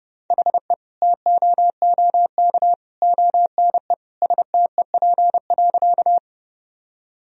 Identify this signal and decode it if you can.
Morse code, heard as a keyed tone, about 30 wpm, 700 Hz. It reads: HE TOOK ONE STEP.
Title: Sentences from Top 1000 Words - Review Part 2 - 30wpm